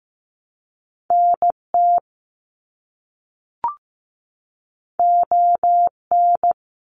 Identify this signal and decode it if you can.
Morse code, heard as a keyed tone, about 15 wpm, 700 Hz. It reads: NT ON
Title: Canadian Provinces and Territories 15wpm